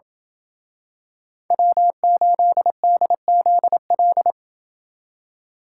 Morse code, heard as a keyed tone, about 28 wpm, 700 Hz. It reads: W8DZL